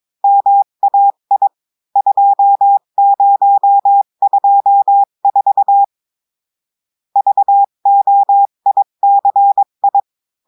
Morse code, heard as a keyed tone, about 22 wpm, 800 Hz. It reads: MAI 2024 VOICI
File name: cours-du-mercredi / audios / cours-853-du-2-mai-2024-etxte chiffre et texte clair-a-20-wpm